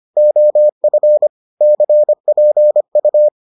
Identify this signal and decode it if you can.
Morse code, heard as a keyed tone, about 25 wpm, 600 Hz. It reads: OF CPU